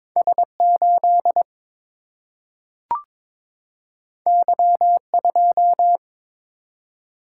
Morse code, heard as a keyed tone, about 22 wpm, 700 Hz. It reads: S 8 Y 2
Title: Single-Letter Single-Number 22wpm